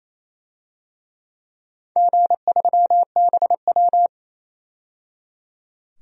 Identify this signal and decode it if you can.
Morse code, heard as a keyed tone, about 28 wpm, 700 Hz. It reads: G3BW
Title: DXCC-100 Call Signs - 4 Characters - 28wpm